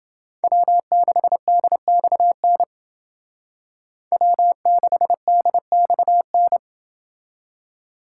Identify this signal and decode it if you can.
Morse code, heard as a keyed tone, about 28 wpm, 700 Hz. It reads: W6DXN W6DXN